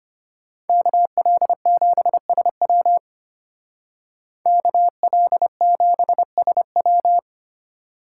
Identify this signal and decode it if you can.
Morse code, heard as a keyed tone, about 33 wpm, 700 Hz. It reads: KL7SW KL7SW